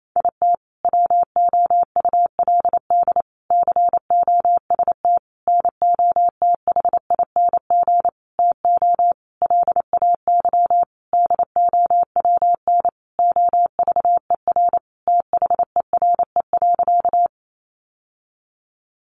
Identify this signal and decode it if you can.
Morse code, heard as a keyed tone, about 28 wpm, 700 Hz. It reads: IT WOULD COST NOTHING TO LAY DOWN OVER THERE.
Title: Sentences from Top 800 Words 28wpm